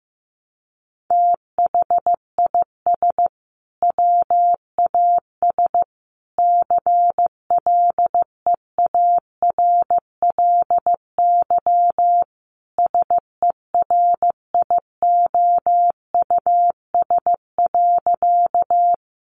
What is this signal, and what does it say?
Morse code, heard as a keyed tone, about 15 wpm, 700 Hz. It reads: THIS WAS CLEARLY SERIOUS.
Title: Sentences from Top 1300 Words 15wpm